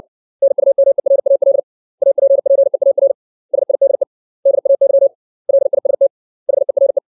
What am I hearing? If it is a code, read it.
Morse code, heard as a keyed tone, about 60 wpm, 550 Hz. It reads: NUMERAL APPEAR SIDE DAY LIST HERE
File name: 632 words / 632 Words 60WPM